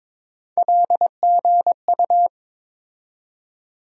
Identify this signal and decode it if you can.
Morse code, heard as a keyed tone, about 22 wpm, 700 Hz. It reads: LGU
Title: USA Call Sign Postfixes - 3 Letters 22wpm